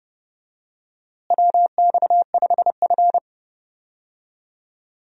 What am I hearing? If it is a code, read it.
Morse code, heard as a keyed tone, about 30 wpm, 700 Hz. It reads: WX5F
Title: USA Call Signs - 4 Characters - Repeated 3x - 30wpm